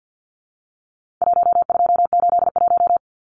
Morse code, heard as a keyed tone, about 49 wpm, 700 Hz. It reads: 1281
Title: Numbers - 4-Digit 50wpm